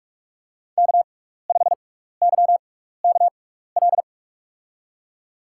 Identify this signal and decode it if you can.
Morse code, heard as a keyed tone, about 46 wpm, spaced as for 15 wpm, 700 Hz. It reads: K5YKL
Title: USA Call Signs - 5 Characters - 6x Character Spacing - 45wpm